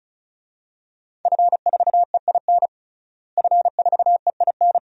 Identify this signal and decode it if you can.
Morse code, heard as a keyed tone, about 36 wpm, 700 Hz. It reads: F4EIN F4EIN
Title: DXCC-100 Call Signs - 5 Characters - Repeated 2x - 35wpm